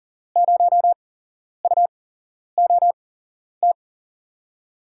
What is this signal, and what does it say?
Morse code, heard as a keyed tone, about 37 wpm, 700 Hz. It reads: 0 U O T
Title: USA Call Signs - 6 Characters - 8x Character Spacing - 40wpm